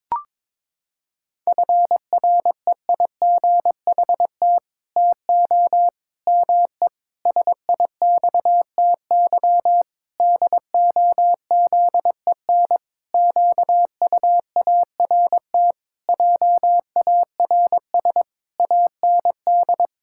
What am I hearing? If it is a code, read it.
Morse code, heard as a keyed tone, about 22 wpm, 700 Hz. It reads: FREIGHT TO ME SIXTY DOZEN QUART JARS AND
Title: Pangram Challenge v2 - 22wpm